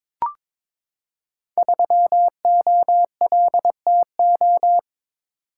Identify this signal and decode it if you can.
Morse code, heard as a keyed tone, about 22 wpm, 700 Hz. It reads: 3OLTO
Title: Code Groups - 5 Letter:Numbers - 22wpm